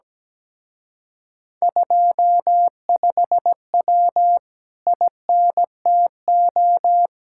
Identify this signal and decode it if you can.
Morse code, heard as a keyed tone, about 17 wpm, 700 Hz. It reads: 25W INTO
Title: Speed-Racing Common QSO Phrases Encore - 17wpm